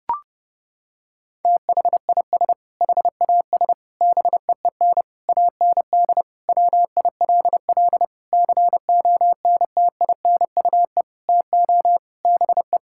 Morse code, heard as a keyed tone, about 30 wpm, 700 Hz. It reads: THIS HAS BEEN AND WILL CONTINUE TO BE
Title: Sentences from Top 1600 Words 30wpm